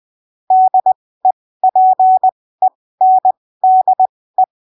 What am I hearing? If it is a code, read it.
Morse code, heard as a keyed tone, about 20 wpm, spaced as for 15 wpm, 750 Hz. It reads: DEPENDE
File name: arquivos / texto / texto 22 20ppm